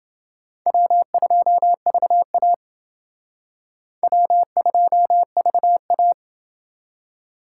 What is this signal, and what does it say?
Morse code, heard as a keyed tone, about 29 wpm, 700 Hz. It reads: W2VA W2VA